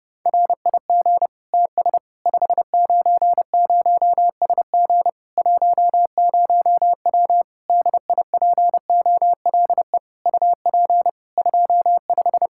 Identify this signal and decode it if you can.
Morse code, heard as a keyed tone, about 29 wpm, 700 Hz. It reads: RIG TS 590SG 10W DIPOLE UP 25
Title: K1USN SST QSO Practice 30wpm